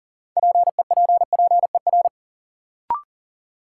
Morse code, heard as a keyed tone, about 40 wpm, 700 Hz. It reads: PEPPER
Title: Sets of 5 Words - New Word Drill - 40wpm